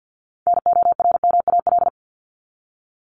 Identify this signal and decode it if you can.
Morse code, heard as a keyed tone, about 50 wpm, 700 Hz. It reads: NORMAL
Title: Top 1001-1250 Words 50wpm